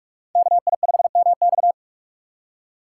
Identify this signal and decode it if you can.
Morse code, heard as a keyed tone, about 45 wpm, 700 Hz. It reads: KI5MX